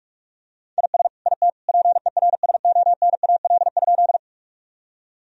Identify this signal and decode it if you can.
Morse code, heard as a keyed tone, about 46 wpm, 700 Hz. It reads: IS IT PERSONAL?
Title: Sentences from Top 1000 Words - Review Part 4 - 45wpm